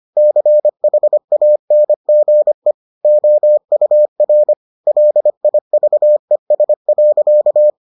Morse code, heard as a keyed tone, about 25 wpm, 600 Hz. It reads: CHANGE OUR LIVES.